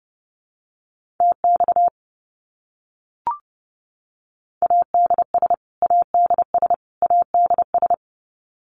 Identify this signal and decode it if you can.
Morse code, heard as a keyed tone, about 30 wpm, 700 Hz. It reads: TX ADS ADS ADS
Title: Abbreviations Repeated 3x - 30wpm